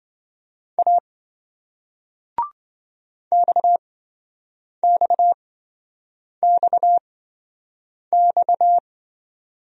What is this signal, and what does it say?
Morse code, heard as a keyed tone, about 27 wpm, spaced as for 7 wpm, 700 Hz. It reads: A XXXX